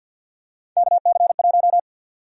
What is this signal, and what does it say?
Morse code, heard as a keyed tone, about 50 wpm, 700 Hz. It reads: KC1
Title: USA Call Sign Prefixes - 50wpm